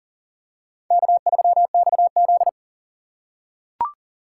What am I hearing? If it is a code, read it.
Morse code, heard as a keyed tone, about 40 wpm, 700 Hz. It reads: K3XZ
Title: USA Call Signs - 4 Characters - 4x Character Spacing - 40wpm